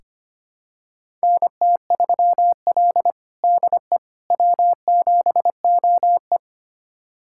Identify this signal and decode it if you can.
Morse code, heard as a keyed tone, about 25 wpm, 700 Hz. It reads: NT3L DE W7OE
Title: USA Callsign DE Callsign - 4 Characters - 25wpm